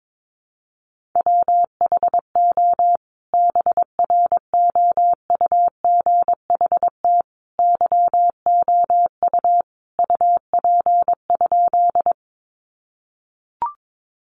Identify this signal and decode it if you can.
Morse code, heard as a keyed tone, about 22 wpm, 700 Hz. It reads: WHO BROUGHT YOU UP?
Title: Sentences from Top 500 Words 22wpm